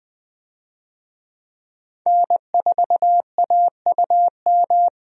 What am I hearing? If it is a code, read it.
Morse code, heard as a keyed tone, about 20 wpm, 700 Hz. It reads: N4AUM
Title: USA Call Signs - 5 Characters - 20wpm